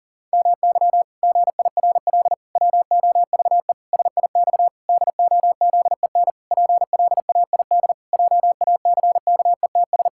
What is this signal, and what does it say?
Morse code, heard as a keyed tone, about 40 wpm, 700 Hz. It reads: MY GIRL WOVE SIX DOZEN PLAID JACKETS
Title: Pangram Challenge v2 - 40wpm